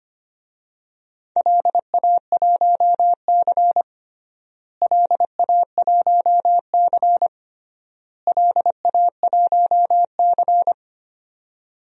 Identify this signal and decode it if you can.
Morse code, heard as a keyed tone, about 25 wpm, 700 Hz. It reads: LA1C LA1C LA1C
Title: DXCC-100 Call Signs - 4 Characters - Repeated 3x - 25wpm